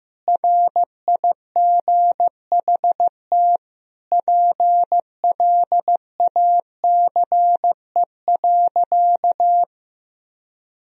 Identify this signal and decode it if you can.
Morse code, heard as a keyed tone, about 15 wpm, 700 Hz. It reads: RIGHT PLACE.
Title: Sentences from Top 1000 Words - Review Part 3 - 15wpm